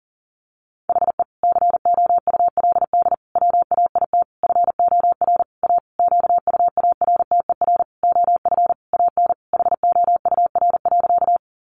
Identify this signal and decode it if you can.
Morse code, heard as a keyed tone, about 40 wpm, 700 Hz. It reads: HE COULD WAIT FOR A QUARTER OF AN HOUR.
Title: Sentences from Top 1300 Words 40wpm